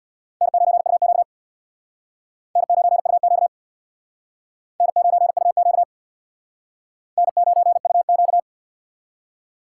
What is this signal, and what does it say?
Morse code, heard as a keyed tone, about 60 wpm, 700 Hz. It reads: N9UQ N9UQ N9UQ N9UQ
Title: Speed Racing - USA 1x2 Call Signs 50wpm